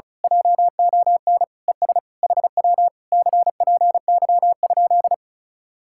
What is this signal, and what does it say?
Morse code, heard as a keyed tone, about 32 wpm, 700 Hz. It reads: JON ES HW CPY?
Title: Speed-Racing Common QSO Phrases Encore - 35wpm